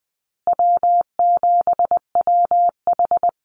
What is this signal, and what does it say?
Morse code, heard as a keyed tone, about 20 wpm, 700 Hz. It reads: W7WH
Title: USA Call Signs Repeated 2x 20wpm